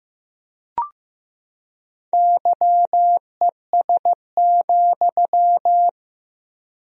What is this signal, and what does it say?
Morse code, heard as a keyed tone, about 15 wpm, 700 Hz. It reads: YES,
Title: 3x Word-Spacing - Sentences from Top 500 Words - 15wpm